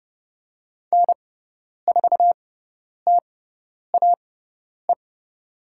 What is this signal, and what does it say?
Morse code, heard as a keyed tone, about 30 wpm, spaced as for 10 wpm, 700 Hz. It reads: N4TAE
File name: Lesson 16g - Letter c - Call Signs 30wpm